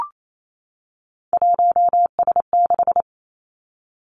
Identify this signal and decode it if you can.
Morse code, heard as a keyed tone, about 28 wpm, 700 Hz. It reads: 1S6
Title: Three Challenging Letter-Number Combinations 28wpm